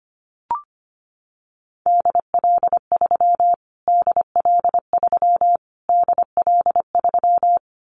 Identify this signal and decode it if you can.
Morse code, heard as a keyed tone, about 25 wpm, 700 Hz. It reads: DL3 DL3 DL3
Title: Callsign Prefixes for Top 25 DXCC - Repeated 3x - 25wpm